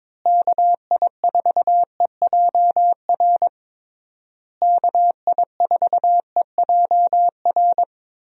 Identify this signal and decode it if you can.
Morse code, heard as a keyed tone, about 22 wpm, 700 Hz. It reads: KI4EJR KI4EJR
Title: USA Call Signs - 6 Characters - Repeated 3x - 22wpm